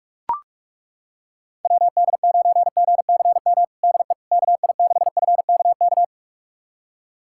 Morse code, heard as a keyed tone, about 41 wpm, 700 Hz. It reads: WD9GCM DE KI6FKK
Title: USA Callsign DE Callsign - 6 Characters - 45wpm